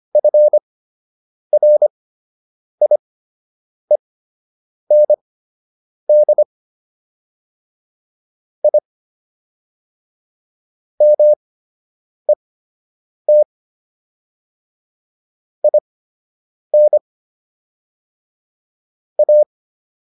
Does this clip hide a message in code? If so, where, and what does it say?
Yes: Morse code from 0.2 to 19.4 s, heard as a keyed tone, about 25 wpm, spaced as for 8 wpm, 600 Hz. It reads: FRIEND I MET IN A